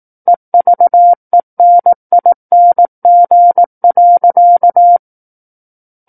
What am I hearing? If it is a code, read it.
Morse code, heard as a keyed tone, about 18 wpm, 700 Hz. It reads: EVENING.
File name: Book One / 09-Chapter 9 - 18 WPM